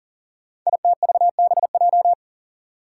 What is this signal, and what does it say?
Morse code, heard as a keyed tone, about 43 wpm, 700 Hz. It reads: ITVBJ
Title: Code Groups - 5 Letters - 40wpm